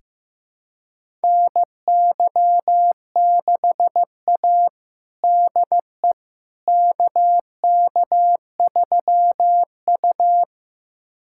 Morse code, heard as a keyed tone, about 15 wpm, 700 Hz. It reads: NY6A DE KK3U
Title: USA Callsign DE Callsign - 4 Characters - 15wpm